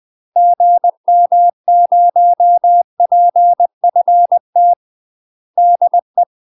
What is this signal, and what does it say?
Morse code, heard as a keyed tone, about 20 wpm, 700 Hz. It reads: GM0PFT DE
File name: UK MASTER QSOs TRAINING / 20WPM / UK MASTER QSOs 0006